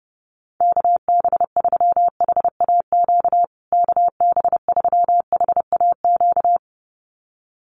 Morse code, heard as a keyed tone, about 30 wpm, 700 Hz. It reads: KB3HAQ KB3HAQ
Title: USA Call Signs Repeated 2x 30wpm